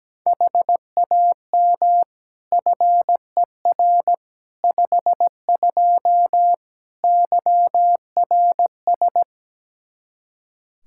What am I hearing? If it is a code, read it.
Morse code, heard as a keyed tone, about 17 wpm, 700 Hz. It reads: HAM FER 52 YRS
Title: Practice QSO - Exchange 3 - More Info - 17wpm